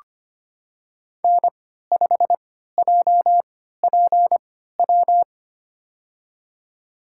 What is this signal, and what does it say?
Morse code, heard as a keyed tone, about 25 wpm, spaced as for 14 wpm, 700 Hz. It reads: N5JPW
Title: USA Call Signs - 3x Character Spacing - 25wpm